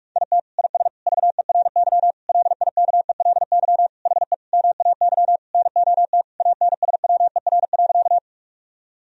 Morse code, heard as a keyed tone, about 41 wpm, 700 Hz. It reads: IT IS VERY LIKELY HE MAY NOT ANSWER.
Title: Sentences from Top 1000 Words - Review Part 4 - 45wpm